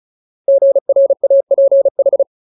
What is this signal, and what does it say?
Morse code, heard as a keyed tone, about 34 wpm, 550 Hz. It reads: GRAPH
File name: Ultimate Word List / 35WPM Ultimate